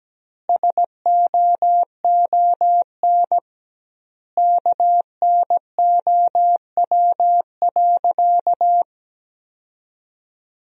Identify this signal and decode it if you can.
Morse code, heard as a keyed tone, about 17 wpm, 700 Hz. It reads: SOON KNOW.
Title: 2x Word-Spacing - Sentences from Top 400 Words - 17wpm